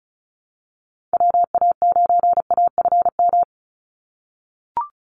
Morse code, heard as a keyed tone, about 36 wpm, 700 Hz. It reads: WA9AFM
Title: USA Call Signs 35wpm